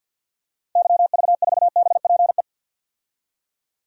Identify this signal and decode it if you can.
Morse code, heard as a keyed tone, about 48 wpm, 700 Hz. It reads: YV4BPE
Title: DXCC-100 Call Signs - 6 Characters - 50wpm